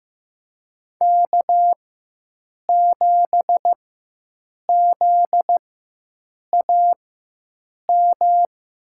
Morse code, heard as a keyed tone, about 15 wpm, spaced as for 7 wpm, 700 Hz. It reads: K7ZAM
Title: USA Call Signs - 5 Characters - 4x Character Spacing - 15wpm